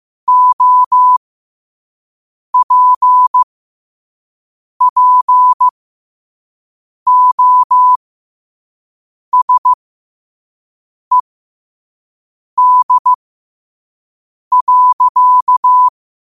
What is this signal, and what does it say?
Morse code, heard as a keyed tone, about 15 wpm, spaced as for 5 wpm, 1000 Hz. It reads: OPPOSED.
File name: qotdfiles / 5wpm